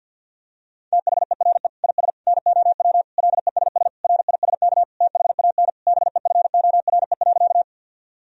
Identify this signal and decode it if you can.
Morse code, heard as a keyed tone, about 50 wpm, 700 Hz. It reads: THERE IS NOW LESS RISK THAN BEFORE.
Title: Sentences from Top 1000 Words - Review Part 4 - 50wpm